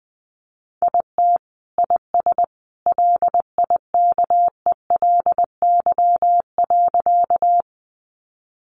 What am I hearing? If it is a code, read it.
Morse code, heard as a keyed tone, about 20 wpm, 700 Hz. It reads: IT IS LIKELY.
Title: Sentences from Top 600 Words 20wpm